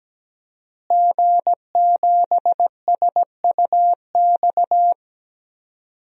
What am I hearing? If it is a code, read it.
Morse code, heard as a keyed tone, about 17 wpm, 700 Hz. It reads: G7SUX
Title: DXCC-100 Call Signs - 5 Characters - Repeated 2x - 17wpm